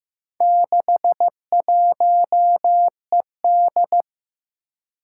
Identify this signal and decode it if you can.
Morse code, heard as a keyed tone, about 15 wpm, 700 Hz. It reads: 61ED
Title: Four Challenging Letter-Number Combinations 15wpm